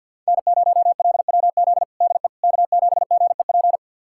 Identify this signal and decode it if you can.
Morse code, heard as a keyed tone, about 46 wpm, 700 Hz. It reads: N0LWZ DE K7GEP
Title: USA Callsign DE Callsign - 5 Characters - 50wpm